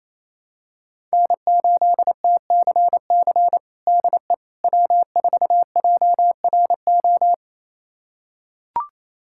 Morse code, heard as a keyed tone, about 28 wpm, 700 Hz. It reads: N8TCC DE W4JRO
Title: USA Callsign DE Callsign - 5 Characters - 28wpm